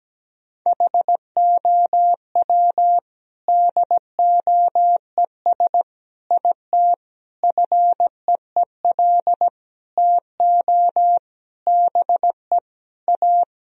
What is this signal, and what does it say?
Morse code, heard as a keyed tone, about 17 wpm, 700 Hz. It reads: HOW DOES IT FEEL TO BE A
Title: Sentences from Top 1000 Words - Review Part 2 - 17wpm